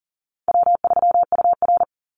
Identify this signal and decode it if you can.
Morse code, heard as a keyed tone, about 40 wpm, 700 Hz. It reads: W3UR
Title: Call Signs Repeated 3x - 40wpm